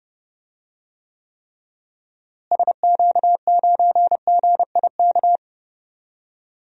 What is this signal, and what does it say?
Morse code, heard as a keyed tone, about 30 wpm, 700 Hz. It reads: SQ9GIK
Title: DXCC-100 Call Signs - 6 Characters - Repeated 3x - 30wpm